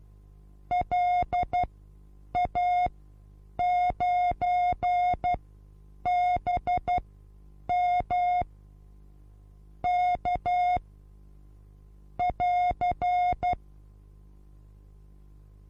Morse code, heard as a keyed tone, about 12 wpm, spaced as for 8 wpm, 720 Hz. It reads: LA9BM K <AR>
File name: Onlinelehrgang / morsen / DJ4UF Morsen23